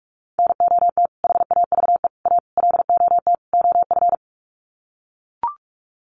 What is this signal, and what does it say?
Morse code, heard as a keyed tone, about 44 wpm, 700 Hz. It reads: NOT HAVE A LOT OF
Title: Sets of 5 Words 45wpm